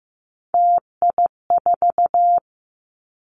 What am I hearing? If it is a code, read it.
Morse code, heard as a keyed tone, about 15 wpm, 700 Hz. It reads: TI4
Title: Callsign Prefixes for Top 100 DXCC - 15wpm